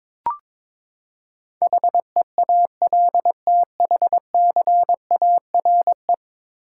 Morse code, heard as a keyed tone, about 22 wpm, 700 Hz. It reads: HEALTHCARE
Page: morse-code-ninja-practice-22wpm